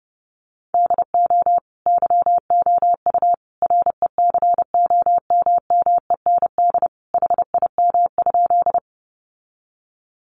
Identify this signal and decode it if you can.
Morse code, heard as a keyed tone, about 30 wpm, 700 Hz. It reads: DO YOU RECOMMEND HIM?
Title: Sentences from Top 1200 Words 30wpm